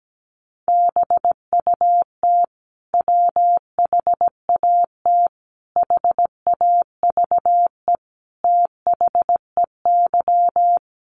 Morse code, heard as a keyed tone, about 17 wpm, 700 Hz. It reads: BUT WHAT HAVE THEY
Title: Sentences from Top 100 Words 17wpm